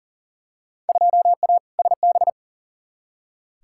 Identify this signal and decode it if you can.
Morse code, heard as a keyed tone, about 40 wpm, 700 Hz. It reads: 2A SB
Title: Field Day Exchanges - 40wpm